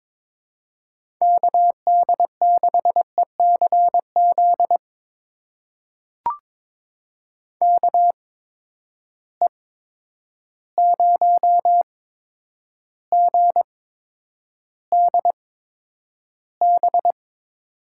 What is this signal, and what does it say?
Morse code, heard as a keyed tone, about 22 wpm, spaced as for 9 wpm, 700 Hz. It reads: KD6ECZ KE0GDB
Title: USA Call Signs - 6 Characters - 8x Character Spacing - 22wpm